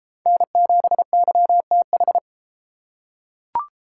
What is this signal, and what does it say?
Morse code, heard as a keyed tone, about 32 wpm, 700 Hz. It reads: N7YTH